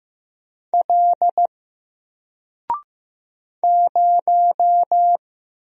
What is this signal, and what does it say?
Morse code, heard as a keyed tone, about 15 wpm, 700 Hz. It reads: L 0
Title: Single Character (Letter:Number:Punctuation:Prosign) 15wpm